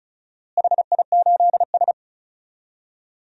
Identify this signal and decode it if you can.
Morse code, heard as a keyed tone, about 34 wpm, 700 Hz. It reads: HI8S
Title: DXCC-100 Call Signs - 4 Characters - Repeated 3x - 35wpm